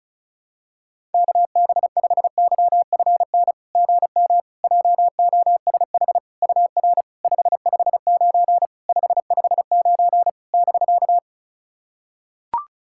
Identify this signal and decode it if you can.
Morse code, heard as a keyed tone, about 36 wpm, 690 Hz. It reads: KB5YFN GM JOSH UR 559 559 <BK>
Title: SOTA QSO Practice 35wpm